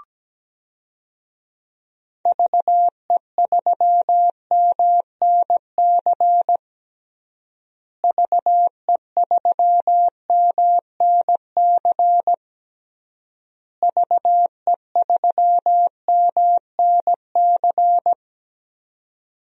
Morse code, heard as a keyed tone, about 17 wpm, 700 Hz. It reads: VE3MNC VE3MNC VE3MNC
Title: DXCC-100 Call Signs - 6 Characters - Repeated 3x - 17wpm